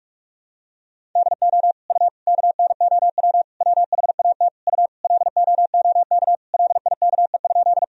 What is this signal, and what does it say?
Morse code, heard as a keyed tone, about 47 wpm, 700 Hz. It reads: DO U KNOW WHAT U LOOK LIKE?
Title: Sentences from Top 100 Words and Abbreviations 45wpm